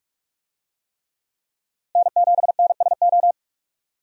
Morse code, heard as a keyed tone, about 46 wpm, 700 Hz. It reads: N7NSO
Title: USA Call Signs - 5 Characters - 45wpm